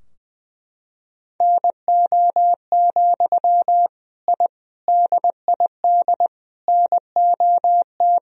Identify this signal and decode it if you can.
Morse code, heard as a keyed tone, about 20 wpm, 700 Hz. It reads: NO, I DID NOT
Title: Sentences from Top 1000 Words - Review Part 2 - 20wpm